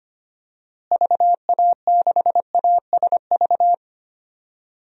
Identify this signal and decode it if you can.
Morse code, heard as a keyed tone, about 25 wpm, 700 Hz. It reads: VA6ASV
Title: DXCC-100 Call Signs - 6 Characters - Repeated 2x - 25wpm